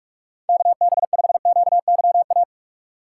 Morse code, heard as a keyed tone, about 43 wpm, 700 Hz. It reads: KB5QYA
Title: USA Call Signs - 6 Characters - 4x Character Spacing - 45wpm